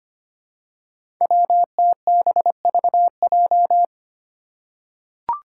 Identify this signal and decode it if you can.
Morse code, heard as a keyed tone, about 25 wpm, 700 Hz. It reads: WTBVJ
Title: Five Challenging Letter-Number Combinations 25wpm